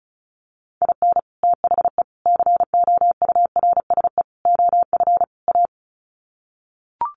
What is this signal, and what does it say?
Morse code, heard as a keyed tone, about 34 wpm, 700 Hz. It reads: IN THE COURSE OF A
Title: Sets of 5 Words Encore2 35wpm